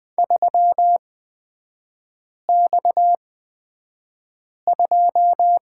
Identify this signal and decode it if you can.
Morse code, heard as a keyed tone, about 20 wpm, spaced as for 5 wpm, 700 Hz. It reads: 3X2
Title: 250 Lesson 40e - BK - Three Characters 20wpm